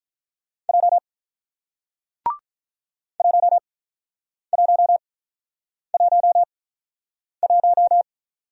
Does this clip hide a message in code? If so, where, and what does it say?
Yes: Morse code from 0.7 to 8.0 s, heard as a keyed tone, about 48 wpm, spaced as for 9 wpm, 700 Hz. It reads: J 1111